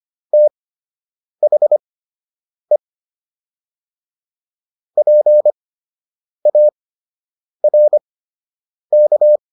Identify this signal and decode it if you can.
Morse code, heard as a keyed tone, about 25 wpm, spaced as for 8 wpm, 600 Hz. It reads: THE PARK